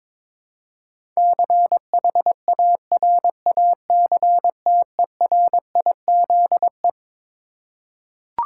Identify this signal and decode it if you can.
Morse code, heard as a keyed tone, about 21 wpm, 700 Hz. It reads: CHARACTERIZE
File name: Twelve Letter Words - 22wpm